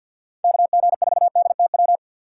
Morse code, heard as a keyed tone, about 51 wpm, 700 Hz. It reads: KG4DTW
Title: USA Call Signs - 3x Character Spacing - 50wpm